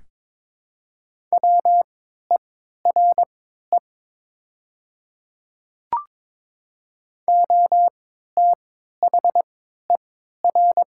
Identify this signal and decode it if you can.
Morse code, heard as a keyed tone, about 22 wpm, spaced as for 12 wpm, 700 Hz. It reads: WERE OTHER
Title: 3x Character Spacing - Top 100 Words - 22wpm